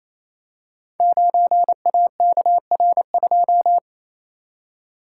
Morse code, heard as a keyed tone, about 28 wpm, 700 Hz. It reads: 9AKR2
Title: Code Groups - 5 Letter:Numbers - 28wpm